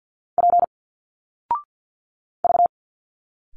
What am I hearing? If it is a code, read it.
Morse code, heard as a keyed tone, about 50 wpm, 700 Hz. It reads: P V
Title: Single Letters 50wpm